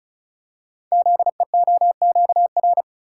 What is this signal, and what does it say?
Morse code, heard as a keyed tone, about 36 wpm, 690 Hz. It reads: ZEOQR
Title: Code Groups - 5 Letters - 35wpm